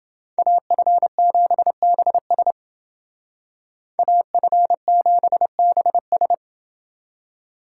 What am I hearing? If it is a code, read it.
Morse code, heard as a keyed tone, about 30 wpm, 700 Hz. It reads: AF7BS AF7BS